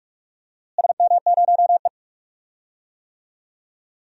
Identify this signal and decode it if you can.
Morse code, heard as a keyed tone, about 46 wpm, 700 Hz. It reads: SM0E
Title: DXCC-100 Call Signs - 4 Characters - Repeated 3x - 45wpm